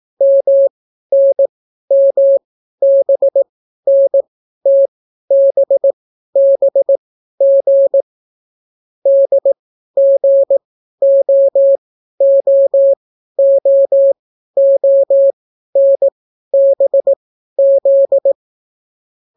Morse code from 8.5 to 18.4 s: DGOOOONBZ